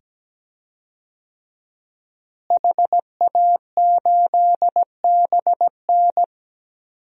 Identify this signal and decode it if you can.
Morse code, heard as a keyed tone, about 17 wpm, 700 Hz. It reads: HA8BN
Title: DXCC-100 Call Signs - 5 Characters - Repeated 2x - 17wpm